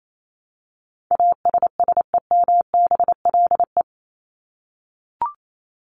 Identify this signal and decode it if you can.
Morse code, heard as a keyed tone, about 28 wpm, 700 Hz. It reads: ASSEMBLE
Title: LY Suffix - Top 100 Words with Base Words 28wpm